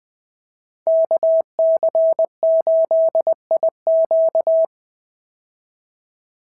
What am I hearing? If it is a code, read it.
Morse code, heard as a keyed tone, about 20 wpm, 650 Hz. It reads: KC8IQ
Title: POTA QSO Practice - 20wpm